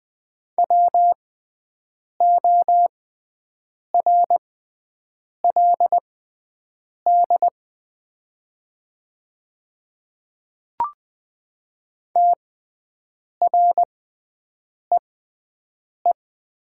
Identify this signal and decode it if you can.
Morse code, heard as a keyed tone, about 20 wpm, spaced as for 7 wpm, 700 Hz. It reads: WORLD TREE
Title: 6x Character Spacing - Top 300 Words - 20wpm